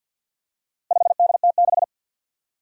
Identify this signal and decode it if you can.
Morse code, heard as a keyed tone, about 47 wpm, 700 Hz. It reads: 5DT6
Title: Four Challenging Letter-Number Combinations 50wpm